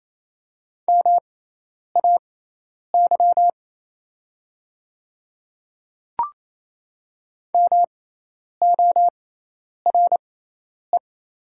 Morse code, heard as a keyed tone, about 27 wpm, spaced as for 10 wpm, 700 Hz. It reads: MAY MORE
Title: 6x Character Spacing - Top 200 Words - 28wpm